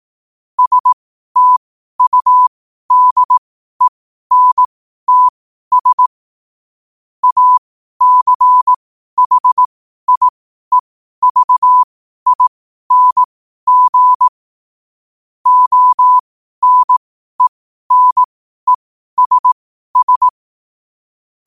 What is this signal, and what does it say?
Morse code, heard as a keyed tone, about 18 wpm, spaced as for 12 wpm, 1000 Hz. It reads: STUDENTS ACHIEVING ONENESS
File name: qotdfiles / 12wpm